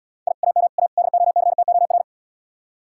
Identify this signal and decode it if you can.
Morse code, heard as a keyed tone, about 76 wpm, 700 Hz. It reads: I AM A DOCTOR